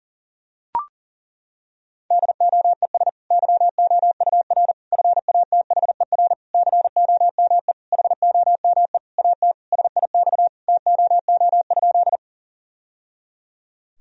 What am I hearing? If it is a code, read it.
Morse code, heard as a keyed tone, about 40 wpm, 700 Hz. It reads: DOES YOUR FATHER COME HOME AT SIX TOO?
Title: Sentences from Top 1000 Words - Review Part 2 - 40wpm